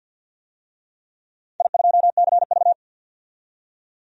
Morse code, heard as a keyed tone, about 47 wpm, 700 Hz. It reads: I2CV
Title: DXCC-100 Call Signs - 4 Characters - 50wpm